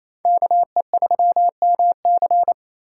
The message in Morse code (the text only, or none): KE3MC